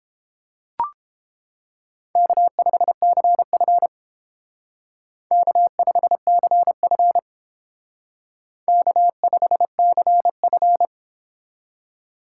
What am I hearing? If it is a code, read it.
Morse code, heard as a keyed tone, about 31 wpm, 700 Hz. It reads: K5CF K5CF K5CF